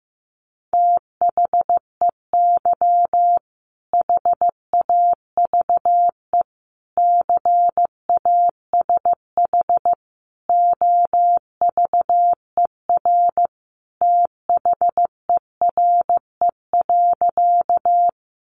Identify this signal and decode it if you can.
Morse code, heard as a keyed tone, about 14 wpm, 700 Hz. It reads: THEY HAVE CASH OVER THERE.
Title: Sentences from Top 700 Words 15wpm